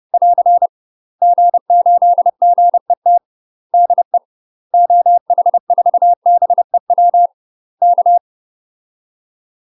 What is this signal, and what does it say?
Morse code, heard as a keyed tone, about 30 wpm, 700 Hz. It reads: <AR> G8GET DE OH4BEW K
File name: UK MASTER QSOs TRAINING / 30WPM / UK MASTER QSOs 0043